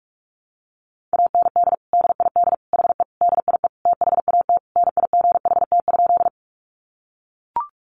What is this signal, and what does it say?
Morse code, heard as a keyed tone, about 44 wpm, 700 Hz. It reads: AND DID HE DIE THAT NIGHT?
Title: Sentences from Top 1300 Words 45wpm